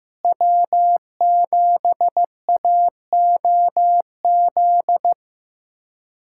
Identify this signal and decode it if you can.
Morse code, heard as a keyed tone, about 16 wpm, 700 Hz. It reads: W7AOZ
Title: USA Callsign DE Callsign - 5 Characters - 15wpm